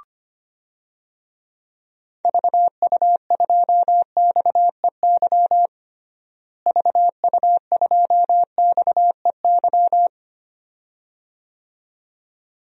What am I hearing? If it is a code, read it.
Morse code, heard as a keyed tone, about 25 wpm, 700 Hz. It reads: VU2XEY VU2XEY
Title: DXCC-100 Call Signs - 6 Characters - Repeated 2x - 25wpm